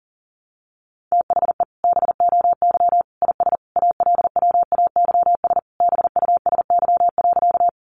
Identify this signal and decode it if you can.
Morse code, heard as a keyed tone, about 40 wpm, 700 Hz. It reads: THE BOY IS ALWAYS BUSY.
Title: Additional Easy Phrases 40wpm